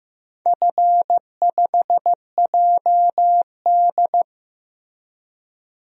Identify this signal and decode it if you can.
Morse code, heard as a keyed tone, about 15 wpm, 700 Hz. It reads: F5JD
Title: DXCC-100 Call Signs - 4 Characters - Repeated 3x - 15wpm